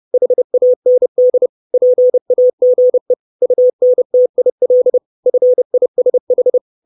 Morse code, heard as a keyed tone, about 30 wpm, 500 Hz. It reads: HAND PAGE UNTIL FISH
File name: 632 Words 30WPM